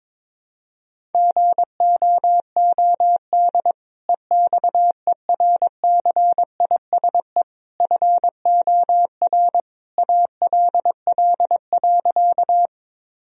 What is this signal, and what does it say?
Morse code, heard as a keyed tone, about 21 wpm, 700 Hz. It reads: GOOD EXERCISE FOR ALL.
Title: Sentences from Top 1000 Words - Review Part 4 - 22wpm